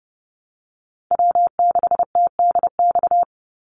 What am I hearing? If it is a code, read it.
Morse code, heard as a keyed tone, about 30 wpm, 700 Hz. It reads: W6TDX
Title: NA SOTA Chaser Callsigns 30wpm